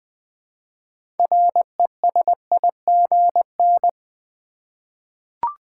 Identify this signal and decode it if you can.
Morse code, heard as a keyed tone, about 20 wpm, 700 Hz. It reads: RESIGN
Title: 6 Letter Words - First 3 Letters Given - 20wpm